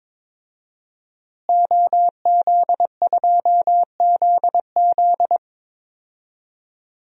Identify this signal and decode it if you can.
Morse code, heard as a keyed tone, about 22 wpm, 700 Hz. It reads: OZ2ZZ
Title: DXCC-100 Call Signs - 5 Characters - Repeated 2x - 22wpm